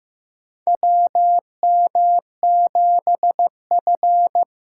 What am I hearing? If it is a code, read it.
Morse code, heard as a keyed tone, about 15 wpm, 700 Hz. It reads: WM7F
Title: USA Call Signs - 4 Characters - 8x Character Spacing - 15wpm